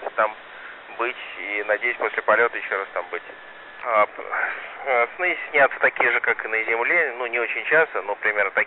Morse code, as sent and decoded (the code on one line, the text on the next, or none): none